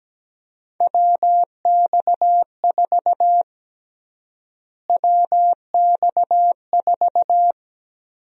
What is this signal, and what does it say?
Morse code, heard as a keyed tone, about 17 wpm, 700 Hz. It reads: WX4 WX4
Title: USA Call Sign Prefixes - Repeated 2x - 17wpm